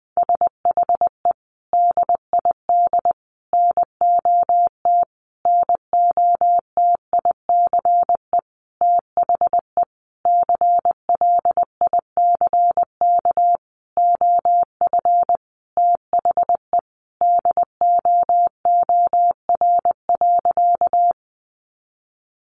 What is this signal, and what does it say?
Morse code, heard as a keyed tone, about 20 wpm, 700 Hz. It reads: SHE DID NOT NOTICE THE CLICK OF THE DOOR.
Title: Sentences from Top 1100 Words 20wpm